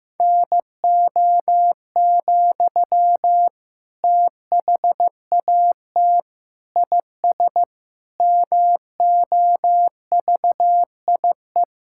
Morse code, heard as a keyed tone, about 16 wpm, 700 Hz. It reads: NO, THAT IS MOVIE
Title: Sentences from Top 1000 Words - Review Part 5 - 15wpm